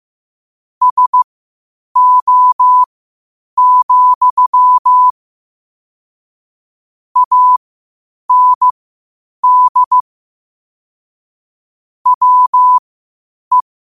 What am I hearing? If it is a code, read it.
Morse code, heard as a keyed tone, about 15 wpm, spaced as for 8 wpm, 1000 Hz. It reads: SO, AND WE
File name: qotdfiles / 8wpm